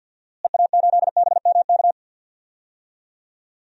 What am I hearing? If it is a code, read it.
Morse code, heard as a keyed tone, about 51 wpm, 700 Hz. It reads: EA8BMK
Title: DXCC-100 Call Signs - 6 Characters - Repeated 2x - 50wpm